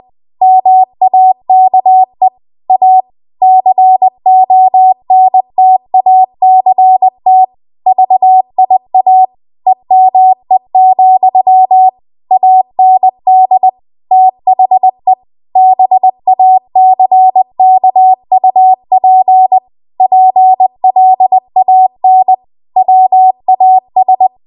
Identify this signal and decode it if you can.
Morse code, heard as a keyed tone, about 20 wpm, 750 Hz. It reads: MAKE A CONTACT VIA EME, AND THE BACKUP PLAN WAS
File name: Morse / 20WPM